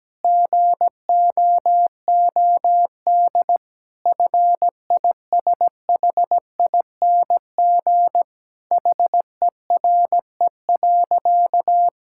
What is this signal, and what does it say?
Morse code, heard as a keyed tone, about 17 wpm, 700 Hz. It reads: GOOD FISHING HERE.
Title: Sentences from Top 1000 Words - Review Part 5 - 17wpm